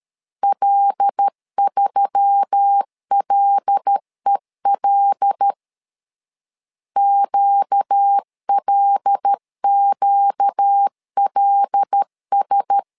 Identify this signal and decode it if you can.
Morse code, heard as a keyed tone, about 13 wpm, 800 Hz. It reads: L3LEL QLQLS
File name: LEZIONI / 5 LBVC / 5 L 12wpm 1913